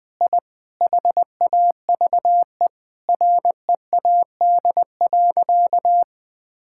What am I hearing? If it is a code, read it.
Morse code, heard as a keyed tone, about 20 wpm, 700 Hz. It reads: I HAVE READ.
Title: Sentences with 5-Word Phrases - 20wpm